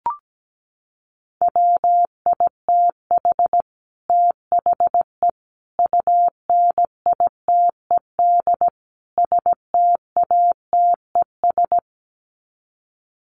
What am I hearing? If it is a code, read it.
Morse code, heard as a keyed tone, about 17 wpm, 700 Hz. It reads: WITH THE UNITED STATES
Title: Sets of 4 Words 17wpm